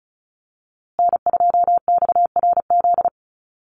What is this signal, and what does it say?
Morse code, heard as a keyed tone, about 34 wpm, 700 Hz. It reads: N2XRZ
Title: USA Call Signs Repeated 3x 35wpm